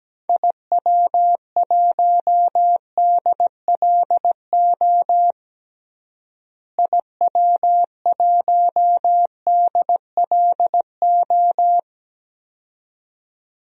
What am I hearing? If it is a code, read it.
Morse code, heard as a keyed tone, about 17 wpm, 700 Hz. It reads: IW1DLO IW1DLO
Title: DXCC-100 Call Signs - 6 Characters - Repeated 2x - 17wpm